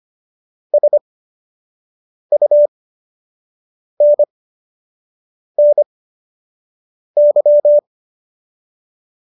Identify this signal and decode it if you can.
Morse code, heard as a keyed tone, about 25 wpm, spaced as for 6 wpm, 590 Hz. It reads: SUNNY